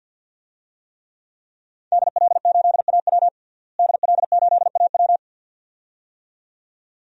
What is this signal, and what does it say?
Morse code, heard as a keyed tone, about 52 wpm, 700 Hz. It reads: DL8AW DL8AW
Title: DXCC-100 Call Signs - 5 Characters - Repeated 2x - 50wpm